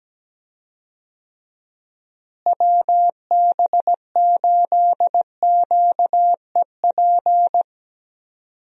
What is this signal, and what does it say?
Morse code, heard as a keyed tone, about 17 wpm, 700 Hz. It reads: WB8QEP
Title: USA Call Signs - 6 Characters - Repeated 2x - 17wpm